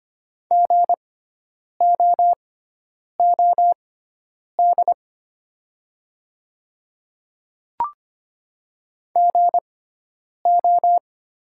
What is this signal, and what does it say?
Morse code, heard as a keyed tone, about 25 wpm, spaced as for 9 wpm, 690 Hz. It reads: GOOD GO
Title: 6x Character Spacing - Top 200 Words - 25wpm